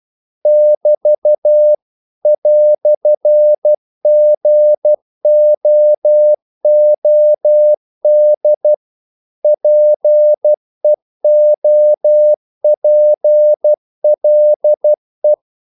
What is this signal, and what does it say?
Morse code, heard as a keyed tone, about 12 wpm, 600 Hz. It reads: <BT> "GOOD PEOPLE
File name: CW / Practice / EnglishQuotations0112